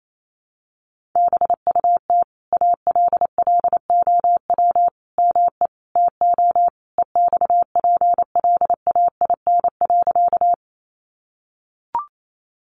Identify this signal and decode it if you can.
Morse code, heard as a keyed tone, about 28 wpm, 700 Hz. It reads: BUT ALLOW ME TO EXPLAIN.
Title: Sentences from Top 1100 Words 28wpm